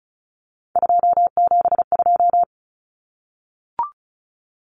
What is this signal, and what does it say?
Morse code, heard as a keyed tone, about 34 wpm, 700 Hz. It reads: 272
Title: Numbers - 3-Digit 35wpm